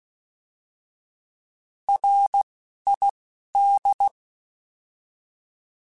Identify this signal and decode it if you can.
Morse code, heard as a keyed tone, about 16 wpm, 800 Hz. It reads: R I D